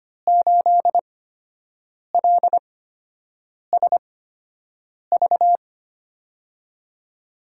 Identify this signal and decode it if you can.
Morse code, heard as a keyed tone, about 25 wpm, spaced as for 7 wpm, 700 Hz. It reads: 8LSV
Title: USA Call Signs - 6 Characters - 8x Character Spacing - 25wpm